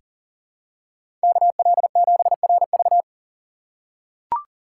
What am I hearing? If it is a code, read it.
Morse code, heard as a keyed tone, about 40 wpm, 700 Hz. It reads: KL7RV
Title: USA Call Signs - 4x Character Spacing - 40wpm